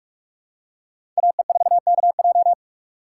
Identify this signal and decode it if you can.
Morse code, heard as a keyed tone, about 47 wpm, 700 Hz. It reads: AE4KJ
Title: USA Call Signs - 2x Character Spacing - 45wpm